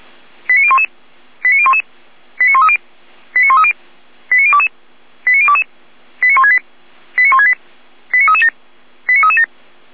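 Five-tone selective call (ZVEI), heard as quick runs of five tones, tones of about 70 ms. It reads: 89011 89011 89130 89130 89022 89022 89168 89168 89337 89337
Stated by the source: ZVEI five-tone selective call 89174 received on SDR.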